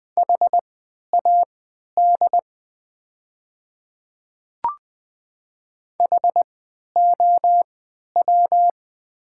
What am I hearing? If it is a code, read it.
Morse code, heard as a keyed tone, about 21 wpm, spaced as for 11 wpm, 700 Hz. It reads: HAD HOW
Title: 3x Character Spacing - Top 100 Words - 20wpm